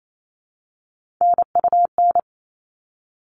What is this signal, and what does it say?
Morse code, heard as a keyed tone, about 28 wpm, 700 Hz. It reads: NUN
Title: All Cut Numbers - 3-Digit 28wpm